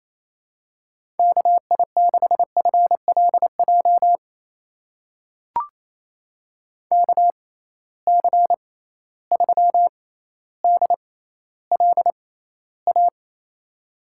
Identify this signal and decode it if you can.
Morse code, heard as a keyed tone, about 28 wpm, spaced as for 14 wpm, 700 Hz. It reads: KI6FLJ KC3DLA
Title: USA Call Signs - 6 Characters - 6x Character Spacing - 28wpm